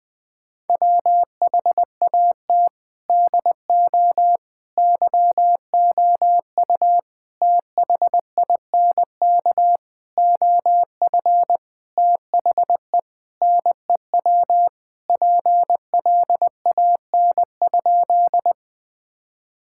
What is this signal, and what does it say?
Morse code, heard as a keyed tone, about 20 wpm, 700 Hz. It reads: WHAT DO YOU THINK OF THE NEW PLAN?
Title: Sentences with 6-Word Phrases - 20wpm